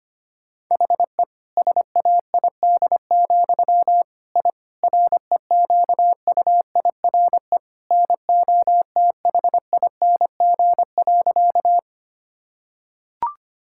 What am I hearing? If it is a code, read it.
Morse code, heard as a keyed tone, about 25 wpm, 700 Hz. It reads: HE SAID, I REQUIRE NOTHING.
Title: Sentences from Top 1000 Words - Review Part 6 - 25wpm